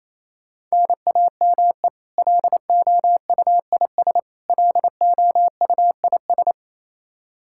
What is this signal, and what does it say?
Morse code, heard as a keyed tone, about 28 wpm, 700 Hz. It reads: NAME LOUIS LOUIS
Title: K1USN SST QSO Practice 28wpm